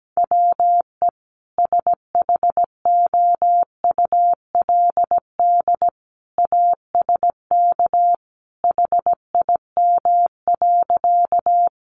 Morse code, heard as a keyed tone, about 17 wpm, 700 Hz. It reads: WE SHOULD ASK HIM.
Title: Sentences from Top 300 Words 17wpm